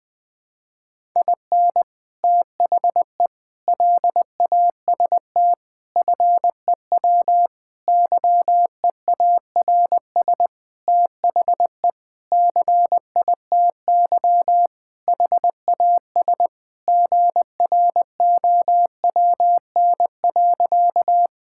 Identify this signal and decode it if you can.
Morse code, heard as a keyed tone, about 20 wpm, 700 Hz. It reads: IN THE LAST FEW YEARS THE CITY HAS GROWN.
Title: Sentences with 5-Word Phrases - Spoken Beginning - 20wpm